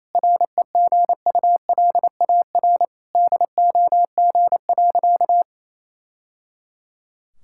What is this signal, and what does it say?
Morse code, heard as a keyed tone, about 28 wpm, 700 Hz. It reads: REGULAR DOG.
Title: Sentences from Top 1000 Words - Review Part 6 - 28wpm